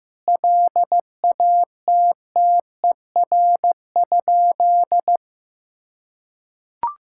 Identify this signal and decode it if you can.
Morse code, heard as a keyed tone, about 15 wpm, 700 Hz. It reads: LATTER?
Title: Sentences from Top 1600 Words 15wpm